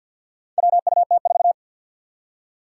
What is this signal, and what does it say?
Morse code, heard as a keyed tone, about 50 wpm, 700 Hz. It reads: WUT4
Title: Four Challenging Letter-Number Combinations 50wpm